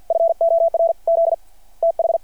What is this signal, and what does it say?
Morse code, heard as a keyed tone, about 52 wpm, 660 Hz. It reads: FOR Z TH
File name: archived / CRSnet